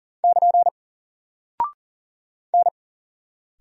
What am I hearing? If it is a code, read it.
Morse code, heard as a keyed tone, about 39 wpm, 700 Hz. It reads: <KN> N
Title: Single Character (Letter:Number:Punctuation:Prosign) Alt 40wpm